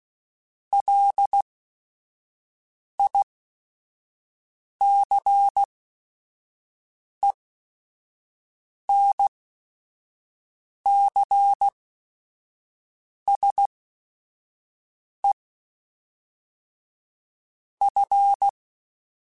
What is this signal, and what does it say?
Morse code, heard as a keyed tone, about 16 wpm, 800 Hz. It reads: L I C E N C S E F
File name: QSOs / QSOs-mp3s / kc7sac@5